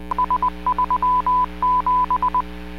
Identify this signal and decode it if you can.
Morse code, heard as a keyed tone, about 20 wpm, 1000 Hz. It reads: S37